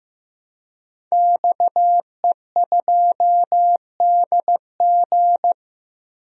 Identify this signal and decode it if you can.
Morse code, heard as a keyed tone, about 15 wpm, 700 Hz. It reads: XE2DG
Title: DXCC-100 Call Signs - 5 Characters - 15wpm